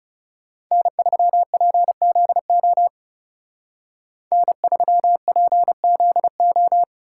Morse code, heard as a keyed tone, about 37 wpm, 700 Hz. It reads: N3PZO N3PZO
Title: Speed Racing - USA Call Signs 30wpm